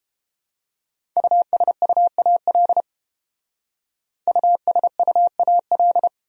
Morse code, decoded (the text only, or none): USUAL USUAL